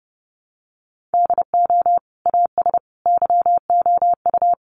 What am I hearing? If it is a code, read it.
Morse code, heard as a keyed tone, about 28 wpm, 700 Hz. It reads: DO AS YOU
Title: Sentences from Top 100 Words 30wpm